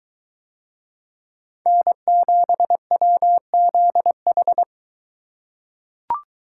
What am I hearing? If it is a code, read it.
Morse code, heard as a keyed tone, about 23 wpm, 700 Hz. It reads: N7WZH